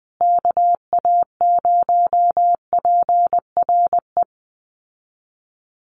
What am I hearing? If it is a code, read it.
Morse code, heard as a keyed tone, about 20 wpm, 700 Hz. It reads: KA0PRE
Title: USA Call Signs Repeated 3x 20wpm